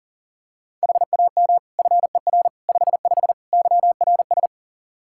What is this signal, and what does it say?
Morse code, heard as a keyed tone, about 40 wpm, 700 Hz. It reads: HAM FER 55 YRS
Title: Practice QSO - Exchange 3 - More Info - 40wpm